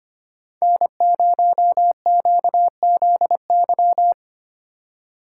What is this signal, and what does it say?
Morse code, heard as a keyed tone, about 25 wpm, 700 Hz. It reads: N0QZY
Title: USA Call Signs - 5 Characters - 3x Character Spacing - 25wpm